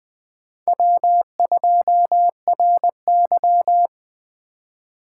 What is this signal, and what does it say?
Morse code, heard as a keyed tone, about 20 wpm, 700 Hz. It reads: W2RY